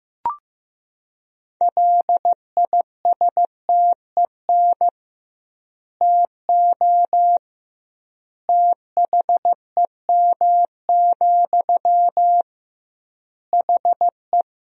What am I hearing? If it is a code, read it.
Morse code, heard as a keyed tone, about 15 wpm, 700 Hz. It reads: LISTEN TO THEM, HE
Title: 2x Word-Spacing - Sentences from Top 400 Words - 15wpm